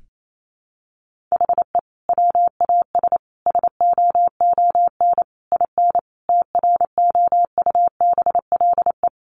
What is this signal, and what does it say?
Morse code, heard as a keyed tone, about 28 wpm, 700 Hz. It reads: HE WAS SOON IN TROUBLE
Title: Sentences from Top 1200 Words 28wpm